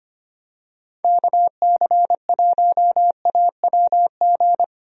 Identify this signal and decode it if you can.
Morse code, heard as a keyed tone, about 25 wpm, 700 Hz. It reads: KC1AWG
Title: USA Call Signs - 6 Characters - Repeated 3x - 25wpm